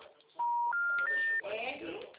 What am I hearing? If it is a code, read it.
Special information tone, with speech.